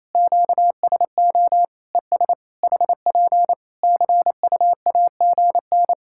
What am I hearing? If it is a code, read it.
Morse code, heard as a keyed tone, about 28 wpm, 700 Hz. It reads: QSO ES HP CUAGN
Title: Practice QSO - Exchange 4 - Ending - 28wpm